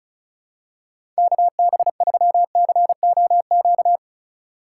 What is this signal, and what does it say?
Morse code, heard as a keyed tone, about 36 wpm, 700 Hz. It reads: KB3COQ
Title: USA Call Signs - 6 Characters - 2x Character Spacing - 35wpm